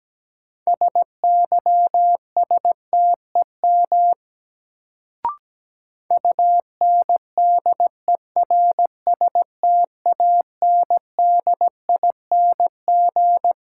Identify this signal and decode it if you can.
Morse code, heard as a keyed tone, about 17 wpm, 700 Hz. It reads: SYSTEM UNDERSTANDING
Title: Sentences from Top 1000 Words - Review Part 3 - New Word Drill - 17wpm